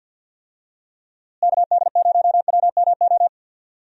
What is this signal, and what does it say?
Morse code, heard as a keyed tone, about 51 wpm, 700 Hz. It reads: KD0WMO
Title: USA Call Signs - 6 Characters - Repeated 3x - 50wpm